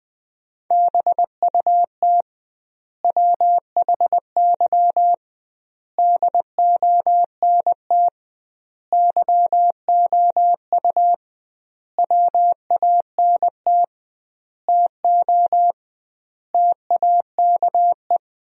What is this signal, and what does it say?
Morse code, heard as a keyed tone, about 20 wpm, 700 Hz. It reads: BUT WHY DONT YOU WANT TO TAKE
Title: 2x Word-Spacing - Sentences from Top 300 Words - 20wpm